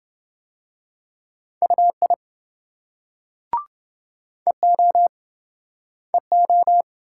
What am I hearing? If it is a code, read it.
Morse code, heard as a keyed tone, about 26 wpm, 700 Hz. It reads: UI EO EO